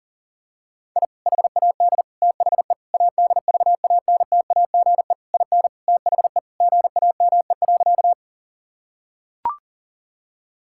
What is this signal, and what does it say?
Morse code, heard as a keyed tone, about 40 wpm, 700 Hz. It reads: I HAD THE ADVANTAGE IN THE GAME.
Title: Sentences from Top 1000 Words - Review Part 6 - 40wpm